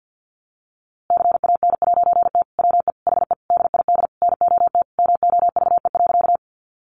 Morse code, heard as a keyed tone, about 50 wpm, 700 Hz. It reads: CAN'T WE HE DID NOT MOVE.
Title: A Sound Of Thunder 50wpm